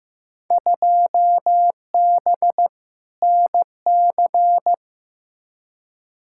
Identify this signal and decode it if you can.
Morse code, heard as a keyed tone, about 15 wpm, 700 Hz. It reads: 2B NC
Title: Field Day Exchanges Encore - 15wpm